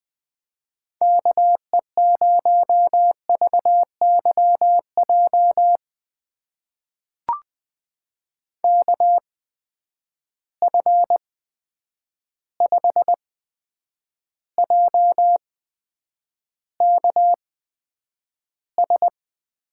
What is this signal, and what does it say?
Morse code, heard as a keyed tone, about 20 wpm, spaced as for 9 wpm, 700 Hz. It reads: KE0VYJ KF5JKS
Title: USA Call Signs - 6 Characters - 8x Character Spacing - 20wpm